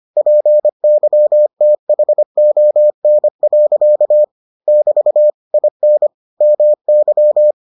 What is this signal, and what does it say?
Morse code, heard as a keyed tone, about 25 wpm, 600 Hz. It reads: PYTHON. <BT> IN MY